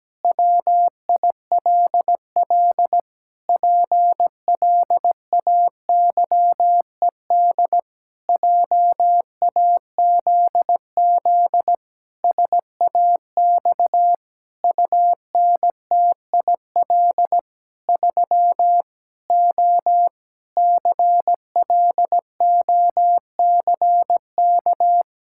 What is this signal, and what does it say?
Morse code, heard as a keyed tone, about 17 wpm, 700 Hz. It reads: WILL PLAYED JAZZ SAX UNTIL 3 O CLOCK
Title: Pangram Challenge v2 - 17wpm